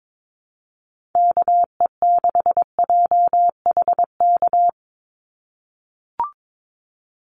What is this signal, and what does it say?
Morse code, heard as a keyed tone, about 22 wpm, 700 Hz. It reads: KE6JHK
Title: USA Call Signs 22wpm